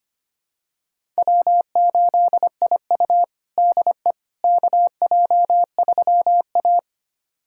Morse code, heard as a keyed tone, about 25 wpm, 700 Hz. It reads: W8IU DE KJ3A
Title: USA Callsign DE Callsign - 4 Characters - 25wpm